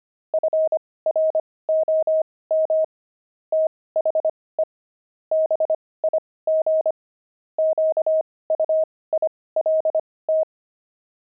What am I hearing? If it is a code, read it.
Morse code, heard as a keyed tone, about 25 wpm, spaced as for 18 wpm, 630 Hz. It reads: FROM THE BIG QUILT
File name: pangram-fs-18-cs-25